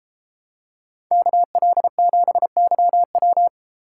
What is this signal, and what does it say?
Morse code, heard as a keyed tone, about 33 wpm, 700 Hz. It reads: KL7YW